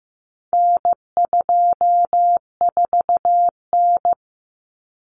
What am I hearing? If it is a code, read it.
Morse code, heard as a keyed tone, about 15 wpm, 700 Hz. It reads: N24N
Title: Common Cut Numbers - 4-Digit 15wpm